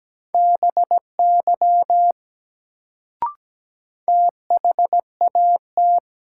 Morse code, heard as a keyed tone, about 17 wpm, 700 Hz. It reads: BY THAT
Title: Sentences from Top 100 Words - New Word Drill - 17wpm